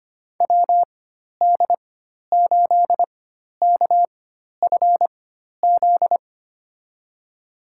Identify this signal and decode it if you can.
Morse code, heard as a keyed tone, about 25 wpm, spaced as for 12 wpm, 700 Hz. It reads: WD8KFZ
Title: USA Call Signs - 6 Characters - 4x Character Spacing - 25wpm